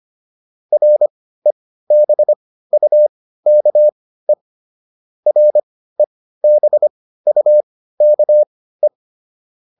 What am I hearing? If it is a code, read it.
Morse code, heard as a keyed tone, about 25 wpm, spaced as for 15 wpm, 600 Hz. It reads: REBUKE REBUKE